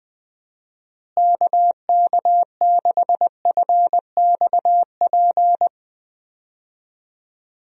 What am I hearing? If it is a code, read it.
Morse code, heard as a keyed tone, about 21 wpm, 700 Hz. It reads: KK6FXP
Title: USA Call Signs - 6 Characters - Repeated 2x - 20wpm